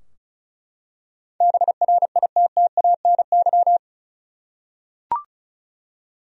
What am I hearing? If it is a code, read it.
Morse code, heard as a keyed tone, about 34 wpm, 700 Hz. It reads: BRITTANY
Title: USA Common Names - 35wpm